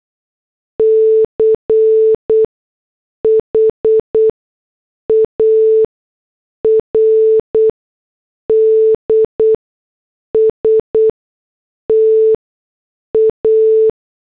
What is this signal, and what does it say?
Morse code, heard as a keyed tone, about 8 wpm, spaced as for 6 wpm, 440 Hz. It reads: CHARD STA